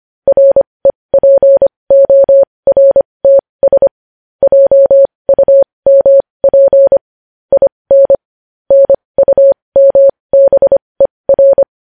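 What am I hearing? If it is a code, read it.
Morse code, heard as a keyed tone, about 25 wpm, spaced as for 20 wpm, 550 Hz. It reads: REPORTS JUMP IN NUMBER
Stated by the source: Morse code sending news headlines at 20 wpm.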